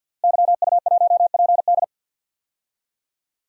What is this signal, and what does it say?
Morse code, heard as a keyed tone, about 52 wpm, 700 Hz. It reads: YU1PD
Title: DXCC-100 Call Signs - 5 Characters - Repeated 2x - 50wpm